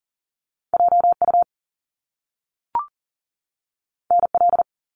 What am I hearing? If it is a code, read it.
Morse code, heard as a keyed tone, about 40 wpm, 700 Hz. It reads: JU NL
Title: Top Two Letter Combinations in Words 40wpm